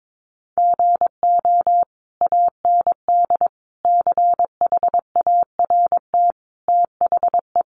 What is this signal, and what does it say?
Morse code, heard as a keyed tone, about 22 wpm, 700 Hz. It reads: GO AND CHART THE
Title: Sentences from Top 1300 Words 22wpm